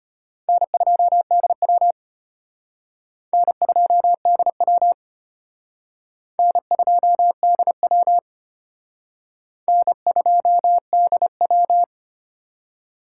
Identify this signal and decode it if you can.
Morse code, heard as a keyed tone, about 32 wpm, 700 Hz. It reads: N2DW N2DW N2DW N2DW